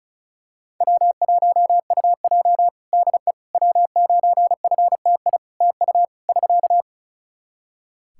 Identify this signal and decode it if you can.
Morse code, heard as a keyed tone, about 34 wpm, 700 Hz. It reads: W1UJ DE W9FTI TU <SK>
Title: Practice QSO - Exchange 4 - Ending - 35wpm